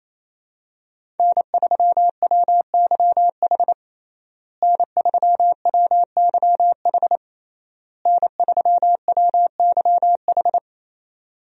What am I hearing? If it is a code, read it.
Morse code, heard as a keyed tone, about 27 wpm, 700 Hz. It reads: N3WYH N3WYH N3WYH
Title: USA Call Signs - 5 Characters - Repeated 3x - 28wpm